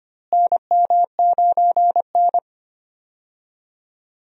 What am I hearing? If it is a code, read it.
Morse code, heard as a keyed tone, about 25 wpm, 700 Hz. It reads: NM9N
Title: USA Call Signs - 4 Characters - Repeated 2x - 25wpm